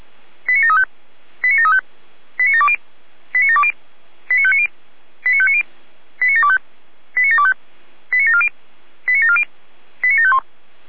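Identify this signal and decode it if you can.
Five-tone selective call (ZVEI), heard as quick runs of five tones, tones of about 70 ms. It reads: 89735 89735 89720 89720 89590 89590 89725 89725 89740 89740 89751
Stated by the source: ZVEI five-tone selective call received on an SDR.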